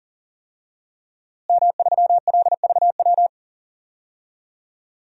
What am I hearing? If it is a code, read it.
Morse code, heard as a keyed tone, about 40 wpm, 700 Hz. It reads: M3LVW
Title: DXCC-100 Call Signs - 5 Characters - Repeated 3x - 40wpm